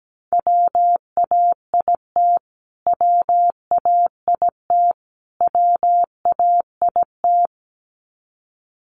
Morse code, heard as a keyed tone, about 17 wpm, 700 Hz. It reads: WAIT WAIT WAIT
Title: Top 500 Words Repeated 17wpm